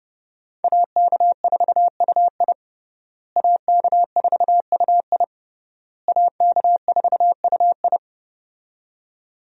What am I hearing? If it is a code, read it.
Morse code, heard as a keyed tone, about 30 wpm, 700 Hz. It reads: AK4UI AK4UI AK4UI
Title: USA Call Signs - 5 Characters - Repeated 3x - 30wpm